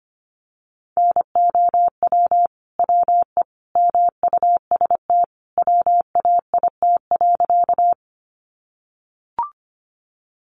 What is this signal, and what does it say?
Morse code, heard as a keyed tone, about 25 wpm, 700 Hz. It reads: NOW WE MUST WAIT.
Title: Sentences from Top 500 Words 25wpm